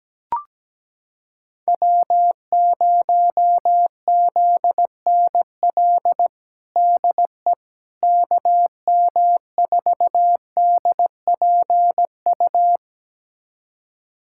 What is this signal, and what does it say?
Morse code, heard as a keyed tone, about 17 wpm, 700 Hz. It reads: W0ZNL DE KM4DPU
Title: USA Callsign DE Callsign 17wpm